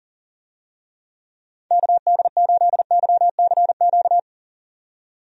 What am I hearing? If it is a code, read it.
Morse code, heard as a keyed tone, about 37 wpm, 700 Hz. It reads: KD8YCQ
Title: USA Call Signs - 6 Characters - 40wpm